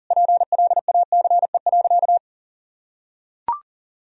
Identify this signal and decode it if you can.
Morse code, heard as a keyed tone, about 40 wpm, 700 Hz. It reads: PLACE.
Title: Sentences from Top 1000 Words - Review Part 1 - 40wpm